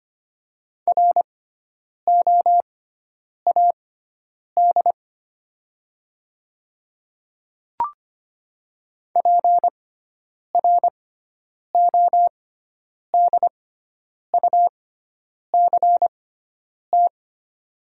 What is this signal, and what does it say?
Morse code, heard as a keyed tone, about 25 wpm, spaced as for 9 wpm, 700 Hz. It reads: ROAD PRODUCT
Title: 6x Character Spacing - Top 500 Words - 25wpm